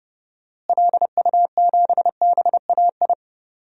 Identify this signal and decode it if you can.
Morse code, heard as a keyed tone, about 30 wpm, 700 Hz. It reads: LU7BAI
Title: DXCC-100 Call Signs - 6 Characters - Repeated 2x - 30wpm